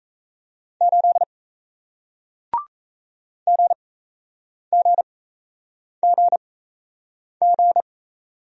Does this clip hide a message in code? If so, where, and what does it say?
Yes: Morse code from 0.8 to 7.8 s, heard as a keyed tone, about 38 wpm, spaced as for 8 wpm, 700 Hz. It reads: 8 GGGG